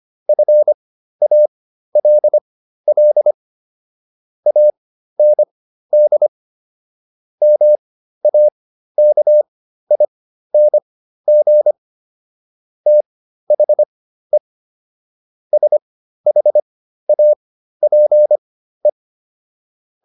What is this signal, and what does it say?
Morse code, heard as a keyed tone, about 25 wpm, spaced as for 13 wpm, 600 Hz. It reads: FALL AND MAKING THE SHAPE